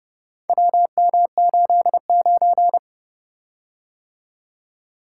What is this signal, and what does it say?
Morse code, heard as a keyed tone, about 30 wpm, 700 Hz. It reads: WM89